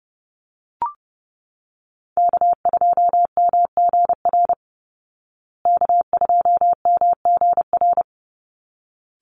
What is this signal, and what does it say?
Morse code, heard as a keyed tone, about 30 wpm, 700 Hz. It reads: K2MGR K2MGR
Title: Call Signs Repeated 3x - 30wpm